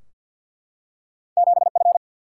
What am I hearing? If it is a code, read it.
Morse code, heard as a keyed tone, about 52 wpm, 700 Hz. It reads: 7 F
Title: Single-Letter Single-Number 50wpm